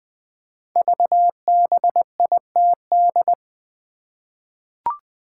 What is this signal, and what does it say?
Morse code, heard as a keyed tone, about 20 wpm, 700 Hz. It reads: VBITD
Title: Code Groups - 5 Letters - 20wpm